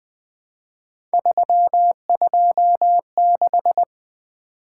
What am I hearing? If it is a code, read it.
Morse code, heard as a keyed tone, about 21 wpm, 700 Hz. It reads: 326
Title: Three Letter-Number Combinations - 20wpm